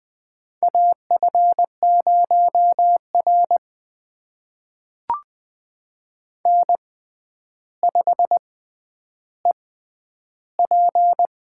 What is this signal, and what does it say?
Morse code, heard as a keyed tone, about 20 wpm, spaced as for 10 wpm, 700 Hz. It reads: AF0R N5EP
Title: USA Call Signs - 4 Characters - 6x Character Spacing - 20wpm